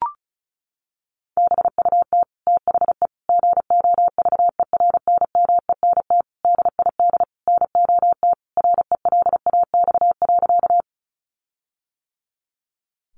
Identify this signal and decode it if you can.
Morse code, heard as a keyed tone, about 34 wpm, 700 Hz. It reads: BUT THE GOVERNMENT DID NOT RELAX.
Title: Sentences from Top 1400 Words 35wpm